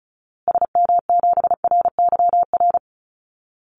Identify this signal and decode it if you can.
Morse code, heard as a keyed tone, about 34 wpm, 700 Hz. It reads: SM7RYR
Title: Call Signs Repeated 2x - 35wpm